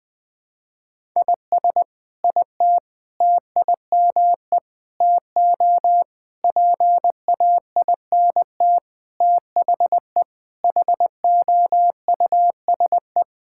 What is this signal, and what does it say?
Morse code, heard as a keyed tone, about 21 wpm, 700 Hz. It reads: IS IT TIME TO PAINT THE HOUSE
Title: Sentences from Top 1000 Words - Review Part 6 - 20wpm